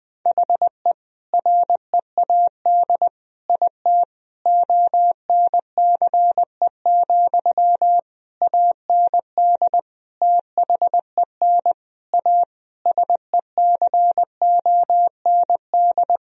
Morse code, heard as a keyed tone, about 20 wpm, 700 Hz. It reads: HE READ IT ONCE, AND THEN A SECOND
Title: Sentences from Top 1000 Words - Review Part 1 - 20wpm